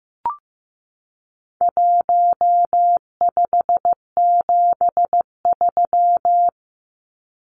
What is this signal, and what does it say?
Morse code, heard as a keyed tone, about 15 wpm, 700 Hz. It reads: 1573
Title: Numbers - Contest Style - 4-Digit 15wpm